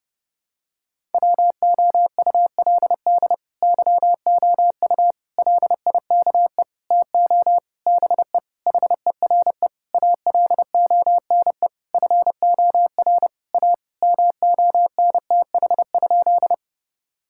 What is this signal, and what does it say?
Morse code, heard as a keyed tone, about 32 wpm, 700 Hz. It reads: WOULD YOU LIKE TO BE HERE ALONE FOR A MONTH?
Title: Sentences from Top 1000 Words - Review Part 5 - 30wpm